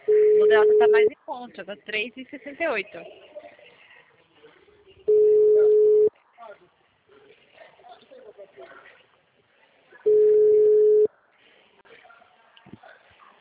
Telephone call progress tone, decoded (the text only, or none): ringback tone